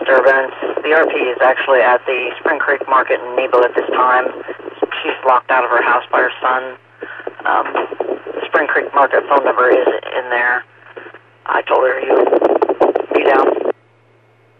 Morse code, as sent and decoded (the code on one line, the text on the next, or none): none